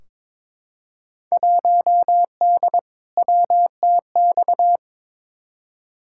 Morse code, heard as a keyed tone, about 22 wpm, 700 Hz. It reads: 1D WTX
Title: Field Day Exchanges - 22wpm